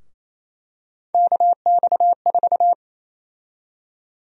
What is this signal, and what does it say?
Morse code, heard as a keyed tone, about 28 wpm, 700 Hz. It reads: KX4
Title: USA Call Sign Prefixes - 28wpm